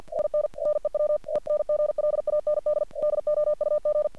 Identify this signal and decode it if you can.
Morse code, heard as a keyed tone, about 45 wpm, 600 Hz. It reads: IN NEW ENGLAND DOUG
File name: archived / CRSnet